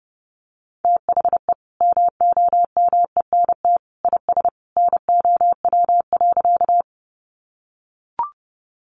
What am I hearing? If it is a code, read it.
Morse code, heard as a keyed tone, about 30 wpm, 700 Hz. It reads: THE MOMENT IS NOW.
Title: Sentences from Top 1000 Words 30wpm